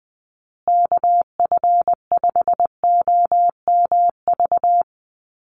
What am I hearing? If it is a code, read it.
Morse code, heard as a keyed tone, about 20 wpm, 700 Hz. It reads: KF5OMV
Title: USA Call Signs Repeated 2x 20wpm